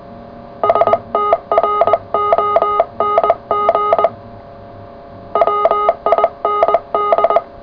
Morse code, heard as a keyed tone, about 20 wpm, 1240 Hz, with noise. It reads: STRONG WIND
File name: full / cw qso